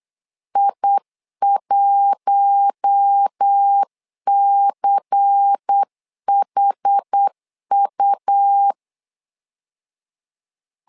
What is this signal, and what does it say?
Morse code, heard as a keyed tone, about 8 wpm, 800 Hz. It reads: I1CHU
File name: LEZIONI / 5 LBVC / 5 LBVC 8wpm 2935